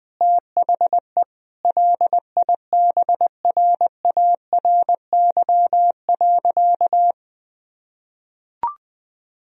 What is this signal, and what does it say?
Morse code, heard as a keyed tone, about 21 wpm, 700 Hz. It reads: THE LIBRARY.
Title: Sentences from Top 1000 Words - Review Part 1 - 20wpm